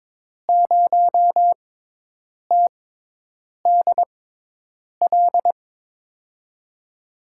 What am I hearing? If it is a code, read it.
Morse code, heard as a keyed tone, about 22 wpm, spaced as for 8 wpm, 700 Hz. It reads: 0TDL
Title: USA Call Signs - 6 Characters - 6x Character Spacing - 22wpm